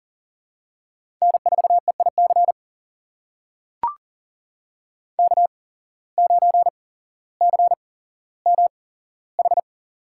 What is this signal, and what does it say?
Morse code, heard as a keyed tone, about 40 wpm, spaced as for 17 wpm, 700 Hz. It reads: N4EIC K9CMH
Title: USA Call Signs - 8x Character Spacing - 40wpm